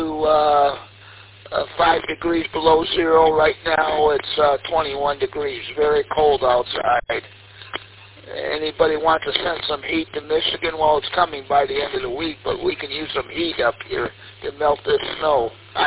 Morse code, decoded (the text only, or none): none